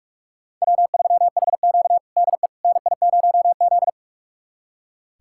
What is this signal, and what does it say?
Morse code, heard as a keyed tone, about 41 wpm, 700 Hz. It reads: W3HQ DE NI0Z
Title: USA Callsign DE Callsign - 4 Characters - 45wpm